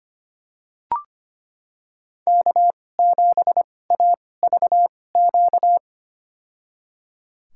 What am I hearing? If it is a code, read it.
Morse code, heard as a keyed tone, about 25 wpm, spaced as for 18 wpm, 700 Hz. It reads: K7AVQ
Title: USA Call Signs - 5 Characters - 2x Character Spacing - 25wpm